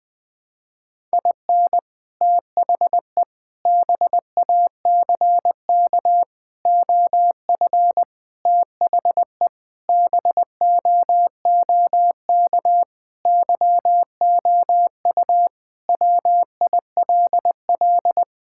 Morse code, heard as a keyed tone, about 20 wpm, 700 Hz. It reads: IN THE BACK OF THE BOOK YOU WILL
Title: Sentences with 5-Word Phrases - Spoken Beginning - 20wpm